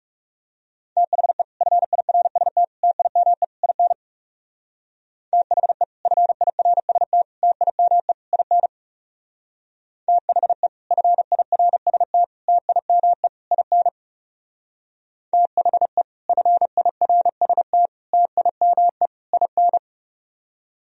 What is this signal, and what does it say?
Morse code, heard as a keyed tone, about 38 wpm, 700 Hz. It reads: THE FIRST TIME IN THE FIRST TIME IN THE FIRST TIME IN THE FIRST TIME IN